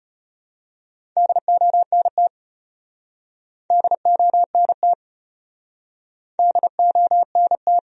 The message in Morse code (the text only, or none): DONT DONT DONT